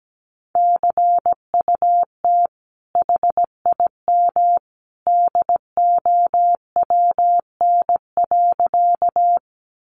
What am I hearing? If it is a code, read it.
Morse code, heard as a keyed tone, about 17 wpm, 700 Hz. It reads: CUT HIM DOWN.
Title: Sentences from Top 300 Words 17wpm